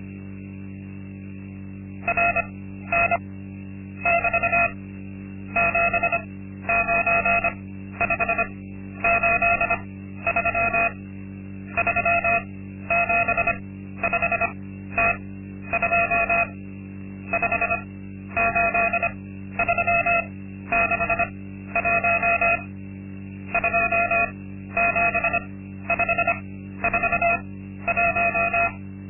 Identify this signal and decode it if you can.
Morse code, heard as a keyed tone, about 25 wpm, spaced as for 12 wpm, 690 Hz. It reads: RN <BT> 79583 375T2 58361 27541
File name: Linksmakalnis-2